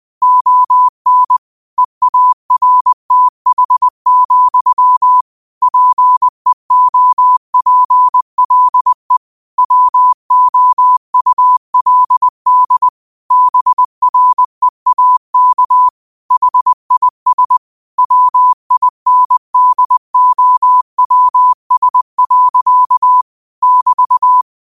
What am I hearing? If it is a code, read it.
Morse code, heard as a keyed tone, about 20 wpm, 1000 Hz. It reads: ON EARTH, PEOPLE WOULD BREAK HIS WINDOWS. <BT>
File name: qotdfiles / 20wpm